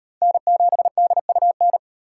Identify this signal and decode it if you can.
Morse code, heard as a keyed tone, about 38 wpm, 700 Hz. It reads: N7DUN